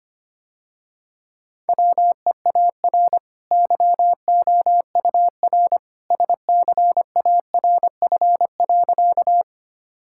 Morse code, heard as a keyed tone, about 25 wpm, 700 Hz. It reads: WEAR YOUR SCARF.
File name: Easy Sentences with Hints - 25wpm